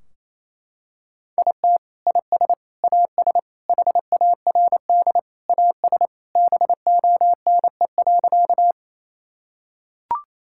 Morse code, heard as a keyed tone, about 28 wpm, 700 Hz. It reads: IT IS AS HARD AS BONE.
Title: Sentences from Top 1600 Words 28wpm